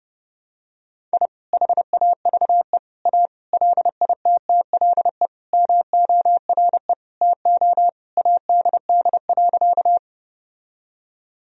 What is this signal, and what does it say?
Morse code, heard as a keyed tone, about 32 wpm, 700 Hz. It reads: I HAVE A LITTLE MORE TO ADD.
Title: Sentences from Top 1000 Words - Review Part 1 - 30wpm